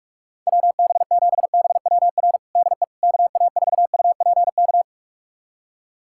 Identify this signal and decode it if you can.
Morse code, heard as a keyed tone, about 46 wpm, 700 Hz. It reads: WB7BWR DE KA4UPK
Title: USA Callsign DE Callsign 45wpm